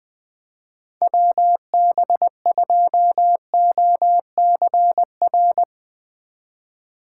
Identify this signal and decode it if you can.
Morse code, heard as a keyed tone, about 20 wpm, 700 Hz. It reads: WB2OCR
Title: Speed Racing - USA 2x3 Call Signs 20wpm